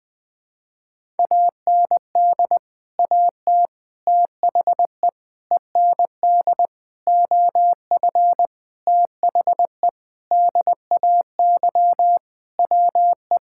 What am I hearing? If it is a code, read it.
Morse code, heard as a keyed tone, about 20 wpm, 700 Hz. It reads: AND AT THE END OF THE DAY WE
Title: Sentences with 5-Word Phrases - Spoken Beginning - 20wpm